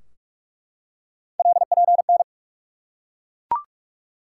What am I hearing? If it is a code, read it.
Morse code, heard as a keyed tone, about 47 wpm, 700 Hz. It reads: LPN
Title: Practice QSO - Exchange 3 - More Info - 45wpm